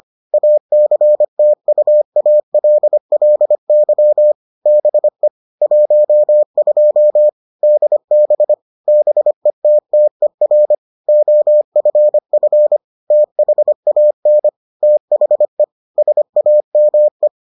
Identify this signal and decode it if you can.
Morse code, heard as a keyed tone, about 25 wpm, 600 Hz. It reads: ACTUALLY BE 12 DB BETTER OFF THAN THE SAME